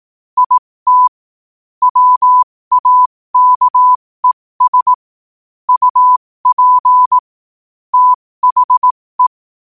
Morse code, heard as a keyed tone, about 18 wpm, spaced as for 15 wpm, 1000 Hz. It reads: IT WAKES UP THE
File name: qotdfiles / 15wpm